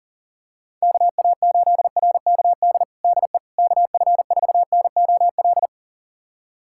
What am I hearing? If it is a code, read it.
Morse code, heard as a keyed tone, about 40 wpm, 700 Hz. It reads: KA8RKD DE KF4NOL
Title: USA Callsign DE Callsign - 6 Characters - 40wpm